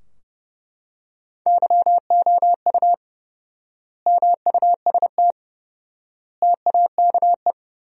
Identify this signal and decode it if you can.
Morse code, heard as a keyed tone, about 30 wpm, 700 Hz. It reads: YOU MUST TAKE
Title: 4x Word-Spacing - Sentences from Top 200 Words - 30wpm